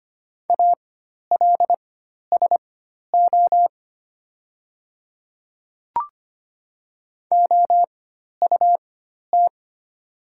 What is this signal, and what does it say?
Morse code, heard as a keyed tone, about 25 wpm, spaced as for 12 wpm, 700 Hz. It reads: ALSO OUT
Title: 4x Character Spacing - Top 200 Words - 25wpm